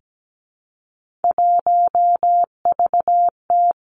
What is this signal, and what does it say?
Morse code, heard as a keyed tone, about 17 wpm, 700 Hz. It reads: 1VT
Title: Three Challenging Letter-Number Combinations 17wpm